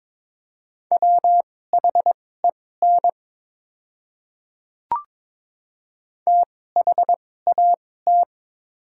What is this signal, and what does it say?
Morse code, heard as a keyed tone, about 22 wpm, spaced as for 16 wpm, 700 Hz. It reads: WHEN THAT
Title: 2x Character Spacing - Top 300 Words - 22wpm